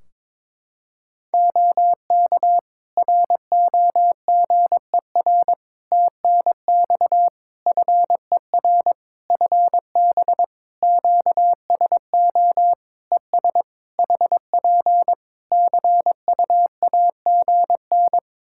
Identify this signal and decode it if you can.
Morse code, heard as a keyed tone, about 22 wpm, 700 Hz. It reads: OK ROGER TNX FER FB QSO ES HP CUAGN
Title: Practice QSO - Exchange 4 - Ending - 22wpm